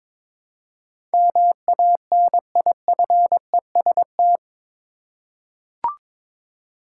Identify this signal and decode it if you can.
Morse code, heard as a keyed tone, about 21 wpm, 700 Hz. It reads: MANIFEST
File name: practice / 22wpm / Eight Letter Words - 22wpm